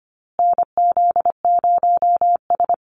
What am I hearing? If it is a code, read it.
Morse code, heard as a keyed tone, about 25 wpm, 700 Hz. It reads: NZ0S
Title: USA Call Signs Repeated 3x 25wpm